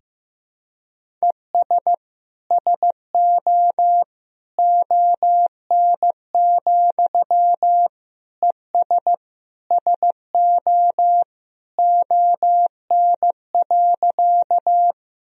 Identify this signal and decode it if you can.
Morse code, heard as a keyed tone, about 15 wpm, 700 Hz. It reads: ES SO ON, ES SO ON.
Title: Sentences from Top 100 Words and Abbreviations 15wpm